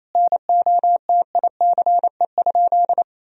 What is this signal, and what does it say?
Morse code, heard as a keyed tone, about 27 wpm, 700 Hz. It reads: NOTICE?
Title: Sentences from Top 1000 Words - Review Part 2 - 28wpm